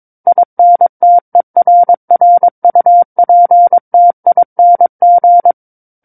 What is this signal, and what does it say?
Morse code, heard as a keyed tone, about 22 wpm, 700 Hz. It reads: INTERRUPTING
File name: Book One / 13-Chapter 13 - 22 WPM